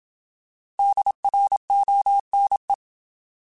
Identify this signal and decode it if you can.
Morse code, heard as a keyed tone, about 27 wpm, 800 Hz. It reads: DRONE